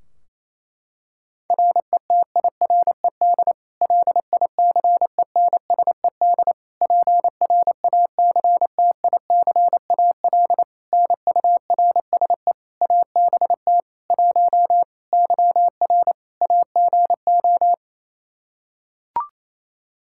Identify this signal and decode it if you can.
Morse code, heard as a keyed tone, about 28 wpm, 700 Hz. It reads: RETIRED LICENSED PRACTICAL NURSE ABT 1 YR AGO
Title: Practice QSO - Exchange 3 - More Info - 28wpm